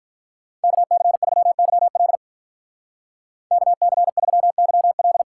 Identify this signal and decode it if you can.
Morse code, heard as a keyed tone, about 45 wpm, 700 Hz. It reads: KC3YL KC3YL